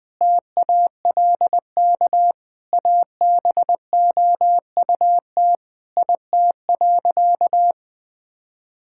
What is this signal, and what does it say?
Morse code, heard as a keyed tone, about 20 wpm, 700 Hz. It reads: TALK ABOUT IT.
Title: Sentences with 5-Word Phrases - 20wpm